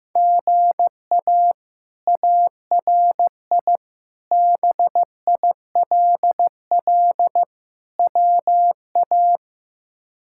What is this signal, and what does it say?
Morse code, heard as a keyed tone, about 15 wpm, 700 Hz. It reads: GA ARI BILL WA
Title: K1USN SST QSO Practice - 15wpm